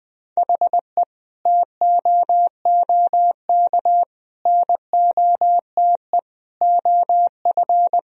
Morse code, heard as a keyed tone, about 20 wpm, 700 Hz. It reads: HE TOOK NOTE OF
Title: Sentences from Top 1000 Words - Review Part 2 - 20wpm